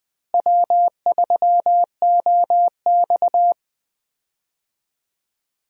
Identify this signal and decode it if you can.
Morse code, heard as a keyed tone, about 20 wpm, 700 Hz. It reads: W3OX
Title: USA Call Signs - 4 Characters - Repeated 2x - 20wpm